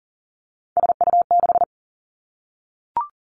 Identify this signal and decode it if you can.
Morse code, heard as a keyed tone, about 38 wpm, 700 Hz. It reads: SU6
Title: Three Challenging Letter-Number Combinations 40wpm